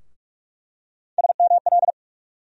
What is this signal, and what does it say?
Morse code, heard as a keyed tone, about 42 wpm, 700 Hz. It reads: SML
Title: USA Call Sign Postfixes - 3 Letters 45wpm